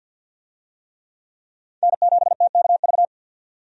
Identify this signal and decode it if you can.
Morse code, heard as a keyed tone, about 50 wpm, 700 Hz. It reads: N7TKV
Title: USA Call Signs - 5 Characters - 50wpm